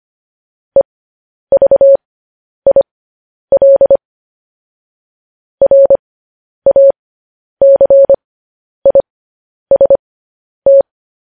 Morse code, heard as a keyed tone, about 25 wpm, spaced as for 10 wpm, 550 Hz. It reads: EVIL RACIST
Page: news-headlines-in-morse-code-at-10-wpm